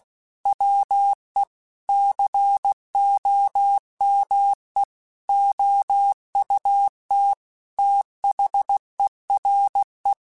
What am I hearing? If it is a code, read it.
Morse code, heard as a keyed tone, about 16 wpm, 800 Hz. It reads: WE COME OUT THERE